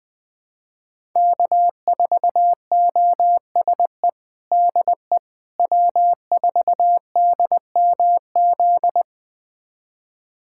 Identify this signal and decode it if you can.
Morse code, heard as a keyed tone, about 21 wpm, 700 Hz. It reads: K4OSE DE W4DMZ
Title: USA Callsign DE Callsign - 5 Characters - 20wpm